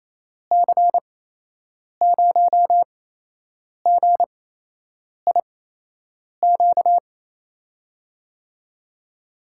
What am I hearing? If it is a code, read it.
Morse code, heard as a keyed tone, about 28 wpm, spaced as for 8 wpm, 700 Hz. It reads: C0GIQ
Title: USA Call Signs - 6 Characters - 8x Character Spacing - 28wpm